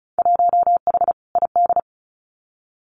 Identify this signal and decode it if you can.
Morse code, heard as a keyed tone, about 34 wpm, 700 Hz. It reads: 1H ID
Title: Winter Field Day Exchanges 35wpm